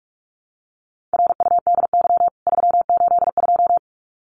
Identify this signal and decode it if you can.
Morse code, heard as a keyed tone, about 44 wpm, 700 Hz. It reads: RUDY 382
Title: CWT Exchange 45wpm